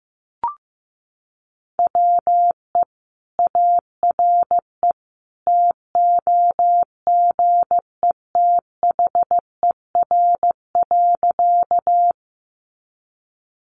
Morse code, heard as a keyed tone, about 15 wpm, 700 Hz. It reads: WE ARE TOGETHER.
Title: Sentences from Top 300 Words 15wpm